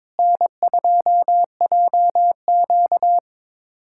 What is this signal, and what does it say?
Morse code, heard as a keyed tone, about 21 wpm, 700 Hz. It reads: N2JQ
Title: USA Callsign DE Callsign - 4 Characters - 22wpm